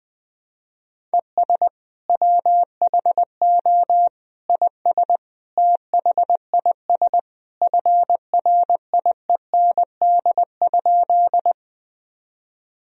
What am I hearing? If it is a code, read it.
Morse code, heard as a keyed tone, about 20 wpm, 700 Hz. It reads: ES WHO IS THIS FRIEND?
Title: Sentences from Top 300 Words and Abbreviations 20wpm